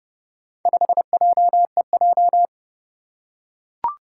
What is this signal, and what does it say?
Morse code, heard as a keyed tone, about 31 wpm, 700 Hz. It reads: 5JEJ
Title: Four Challenging Letter-Number Combinations 30wpm